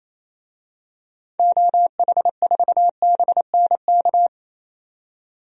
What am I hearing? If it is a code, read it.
Morse code, heard as a keyed tone, about 28 wpm, 700 Hz. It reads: OH4BNK
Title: DXCC-100 Call Signs - 6 Characters - 28wpm